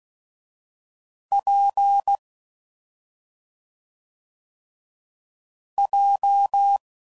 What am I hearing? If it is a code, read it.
Morse code, heard as a keyed tone, about 16 wpm, 800 Hz. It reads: P J
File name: Combinations / Difficult Combos / WPJ WPJ1